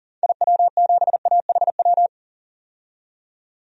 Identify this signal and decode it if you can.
Morse code, heard as a keyed tone, about 37 wpm, 700 Hz. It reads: IW7AHW
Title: DXCC-100 Call Signs - 6 Characters - 40wpm